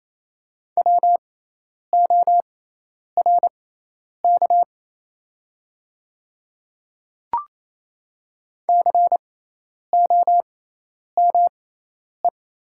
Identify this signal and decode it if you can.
Morse code, heard as a keyed tone, about 28 wpm, spaced as for 10 wpm, 700 Hz. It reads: WORK COME
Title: 6x Character Spacing - Top 200 Words - 28wpm